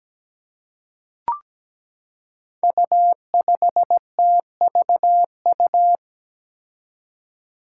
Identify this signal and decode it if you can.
Morse code, heard as a keyed tone, about 17 wpm, 700 Hz. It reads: U5TVU
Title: Five Challenging Letter-Number Combinations 17wpm